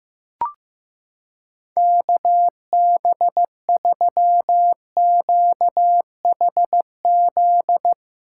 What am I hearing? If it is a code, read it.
Morse code, heard as a keyed tone, about 15 wpm, 700 Hz. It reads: KB3QHZ
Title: USA Callsign DE Callsign - 6 Characters - 15wpm